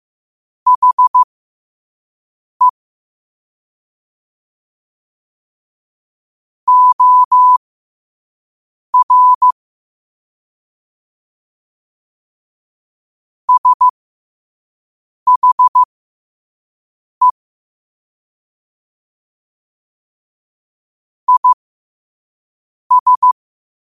Morse code, heard as a keyed tone, about 15 wpm, spaced as for 5 wpm, 1000 Hz. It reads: HE OR SHE IS
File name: qotdfiles / 5wpm